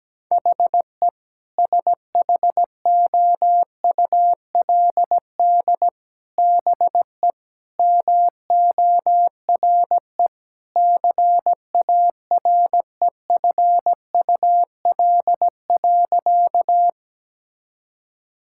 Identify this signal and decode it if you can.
Morse code, heard as a keyed tone, about 17 wpm, 700 Hz. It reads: HE SHOULD BE MORE CAREFUL.
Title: Sentences from Top 1600 Words 17wpm